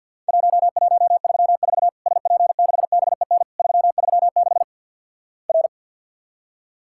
Morse code, heard as a keyed tone, about 50 wpm, 700 Hz. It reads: 1134 SP6BEN 336 R
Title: Worked All Europe DX Contest - QTC Exchange with Repeats - 50wpm